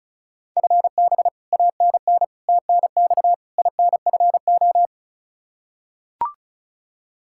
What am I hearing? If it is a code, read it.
Morse code, heard as a keyed tone, about 36 wpm, 700 Hz. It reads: FB ANN TNX INFO
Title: Practice QSO - Exchange 3 - More Info - 35wpm